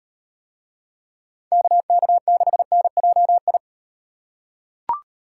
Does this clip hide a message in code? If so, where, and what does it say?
Yes: Morse code from 1.5 to 3.6 s, heard as a keyed tone, about 38 wpm, 700 Hz. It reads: KK6NJI